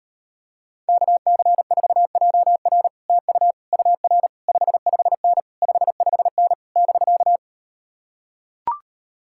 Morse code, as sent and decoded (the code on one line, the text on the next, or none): -.- -.-. ....- .--- .-. / - ..- / ..- .-. / ..... ..... -. / ..... ..... -. / -...-.-
KC4JR TU UR 55N 55N <BK>